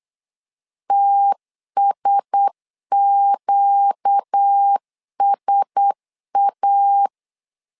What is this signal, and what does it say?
Morse code, heard as a keyed tone, about 8 wpm, 800 Hz. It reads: TSQSA
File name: LEZIONI / 4 Q1S 8wpm 2710